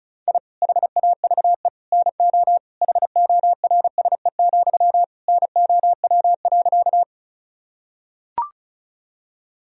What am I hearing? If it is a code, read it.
Morse code, heard as a keyed tone, about 36 wpm, 700 Hz. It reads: I HAVE NO HORSE, NOW.
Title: Sentences from Top 1000 Words - Review Part 1 - 35wpm